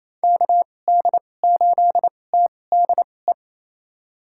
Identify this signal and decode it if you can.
Morse code, heard as a keyed tone, about 28 wpm, spaced as for 20 wpm, 700 Hz. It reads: KD8TDE
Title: USA Call Signs - 6 Characters - 2x Character Spacing - 28wpm